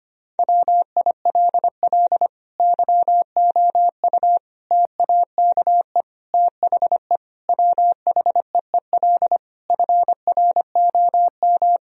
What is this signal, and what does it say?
Morse code, heard as a keyed tone, about 25 wpm, 700 Hz. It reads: WILL YOU TAKE THE WHEEL FROM
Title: Sentences from Top 1000 Words - Review Part 3 - 25wpm